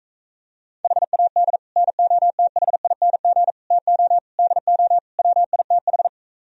Morse code, heard as a keyed tone, about 43 wpm, 700 Hz. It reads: HAD NOTHING TO DO WITH